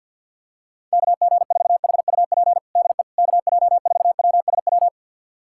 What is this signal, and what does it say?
Morse code, heard as a keyed tone, about 52 wpm, 700 Hz. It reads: KG4HUP DE KJ4WSW
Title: USA Callsign DE Callsign - 6 Characters - 50wpm